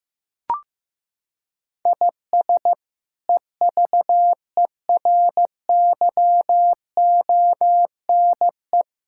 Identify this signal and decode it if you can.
Morse code, heard as a keyed tone, about 15 wpm, 700 Hz. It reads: IS EVERYONE
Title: Sentences from Top 1000 Words - Review Part 5 - 15wpm